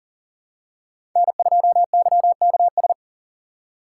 Morse code, heard as a keyed tone, about 40 wpm, 700 Hz. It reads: N2YKS
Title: USA Call Signs - 5 Characters - 4x Character Spacing - 40wpm